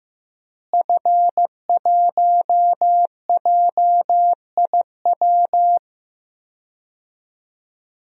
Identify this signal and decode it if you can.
Morse code, heard as a keyed tone, about 15 wpm, 700 Hz. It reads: F1JIW
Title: DXCC-100 Call Signs - 5 Characters - Repeated 3x - 15wpm